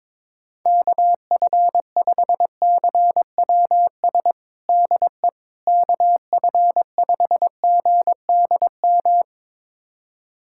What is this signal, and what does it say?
Morse code, heard as a keyed tone, about 22 wpm, 700 Hz. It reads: KF5CWS DE KF5GDM
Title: USA Callsign DE Callsign - 6 Characters - 22wpm